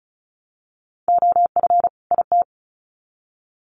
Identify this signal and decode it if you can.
Morse code, heard as a keyed tone, about 34 wpm, 700 Hz. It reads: OF IT
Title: Sets of 2 Words 35wpm